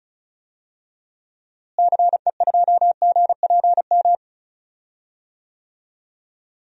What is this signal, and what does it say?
Morse code, heard as a keyed tone, about 34 wpm, 700 Hz. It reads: CE2GPM
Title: DXCC-100 Call Signs - 6 Characters - Repeated 3x - 35wpm